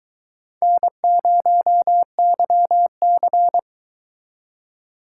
Morse code, heard as a keyed tone, about 22 wpm, 700 Hz. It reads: N0YC